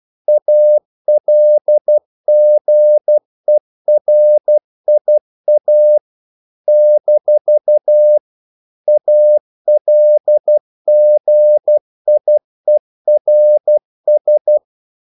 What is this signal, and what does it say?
Morse code, heard as a keyed tone, about 12 wpm, 600 Hz. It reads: ALGERIA - ALGIERS
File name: CW / Practice / NationalCapitals12